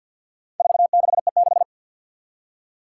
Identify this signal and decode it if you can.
Morse code, heard as a keyed tone, about 52 wpm, 700 Hz. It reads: 46E6
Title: Four Challenging Letter-Number Combinations 50wpm